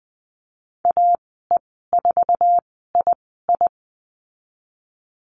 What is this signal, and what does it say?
Morse code, heard as a keyed tone, about 19 wpm, spaced as for 14 wpm, 700 Hz. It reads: AE4II
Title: 048 Lesson 9g - Number 4 - Call Signs 20wpm